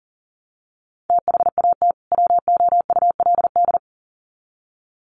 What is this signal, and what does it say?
Morse code, heard as a keyed tone, about 40 wpm, 700 Hz. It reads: THAT WOULD
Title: Sets of 2 Words 40wpm